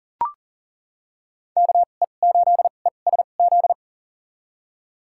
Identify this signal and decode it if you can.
Morse code, heard as a keyed tone, about 40 wpm, spaced as for 29 wpm, 700 Hz. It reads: KE8ESZ
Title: USA Call Signs - 6 Characters - 2x Character Spacing - 40wpm